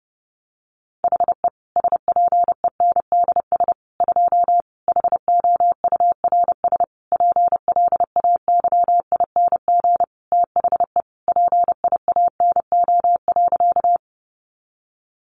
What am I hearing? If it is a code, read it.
Morse code, heard as a keyed tone, about 30 wpm, 700 Hz. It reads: HE SPENDS 2 HOURS PLAYING THE PIANO.
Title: Additional Easy Phrases 30wpm